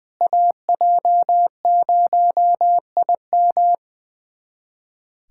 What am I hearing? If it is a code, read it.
Morse code, heard as a keyed tone, about 20 wpm, 700 Hz. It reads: AJ0IM
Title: USA Callsign DE Callsign - 5 Characters - 20wpm